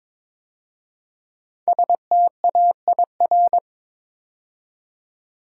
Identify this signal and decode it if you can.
Morse code, heard as a keyed tone, about 23 wpm, 700 Hz. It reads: STAIR
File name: 5 Letter Words - First 3 Letters Given - 22wpm